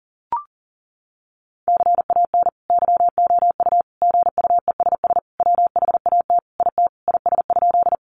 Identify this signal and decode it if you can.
Morse code, heard as a keyed tone, about 40 wpm, 700 Hz. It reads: CAN YOU GUESS WHAT IT IS?
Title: Sentences from Top 1500 Words 40wpm